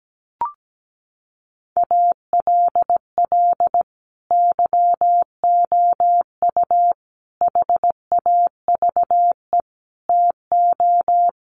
Morse code, heard as a keyed tone, about 17 wpm, 700 Hz. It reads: ALL YOU HAVE TO
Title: Sets of 5 Words 17wpm